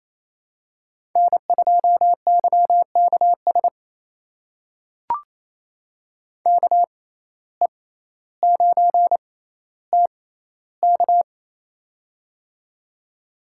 Morse code, heard as a keyed tone, about 27 wpm, spaced as for 14 wpm, 700 Hz. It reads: N2YKS KE9TK
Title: USA Call Signs - 5 Characters - 6x Character Spacing - 28wpm